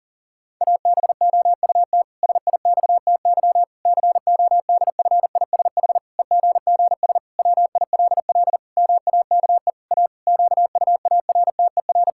Morse code, heard as a keyed tone, about 40 wpm, 700 Hz. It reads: ABOUT SIXTY CODFISH EGGS WILL MAKE A QUARTER
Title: Pangram Challenge v2 - 40wpm